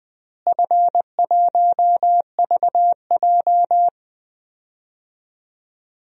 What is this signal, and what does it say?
Morse code, heard as a keyed tone, about 20 wpm, 700 Hz. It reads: F1VJ
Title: DXCC-100 Call Signs - 4 Characters - Repeated 3x - 20wpm